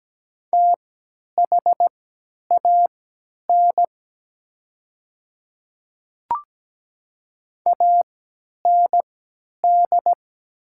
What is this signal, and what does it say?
Morse code, heard as a keyed tone, about 17 wpm, spaced as for 10 wpm, 700 Hz. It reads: THAN AND
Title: 3x Character Spacing - Top 100 Words - 17wpm